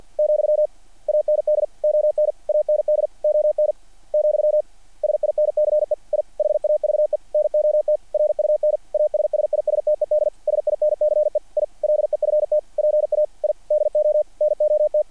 Morse code, heard as a keyed tone, about 48 wpm, 600 Hz. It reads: , AND ON AND ON , SINCE I HAVE NOT RUN ASSISTED SINCE I LEFT PA I DO NOT
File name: archived / CRSnet